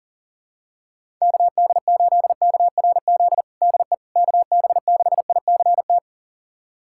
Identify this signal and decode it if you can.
Morse code, heard as a keyed tone, about 40 wpm, 700 Hz. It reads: KD8KRZ DE KB6ICT
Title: USA Callsign DE Callsign 40wpm